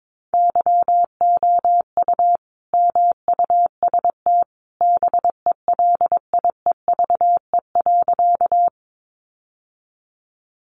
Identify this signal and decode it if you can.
Morse code, heard as a keyed tone, about 22 wpm, 700 Hz. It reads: YOU MUST BELIEVE.
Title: Sentences from Top 1000 Words 22wpm